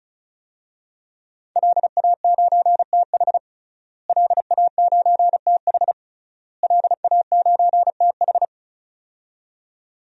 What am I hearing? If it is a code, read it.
Morse code, heard as a keyed tone, about 36 wpm, 700 Hz. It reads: LA9TH LA9TH LA9TH
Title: DXCC-100 Call Signs - 5 Characters - Repeated 3x - 35wpm